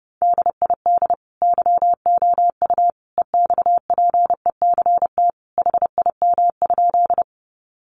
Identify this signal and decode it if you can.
Morse code, heard as a keyed tone, about 30 wpm, 700 Hz. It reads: DID YOU EXPECT HIM?
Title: Sentences from Top 1100 Words 30wpm